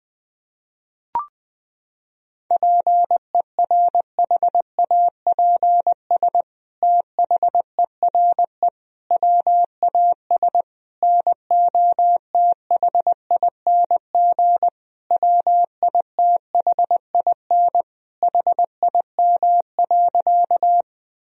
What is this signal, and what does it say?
Morse code, heard as a keyed tone, about 21 wpm, 700 Hz. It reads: PERHAPS THERE WAS NOTHING WITHIN HIM.
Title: Sentences from Top 1000 Words - Review Part 4 - 20wpm